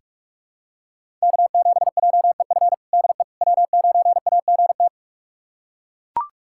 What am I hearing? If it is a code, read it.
Morse code, heard as a keyed tone, about 41 wpm, 700 Hz. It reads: K7JEF DE W9AGT
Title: USA Callsign DE Callsign - 5 Characters - 45wpm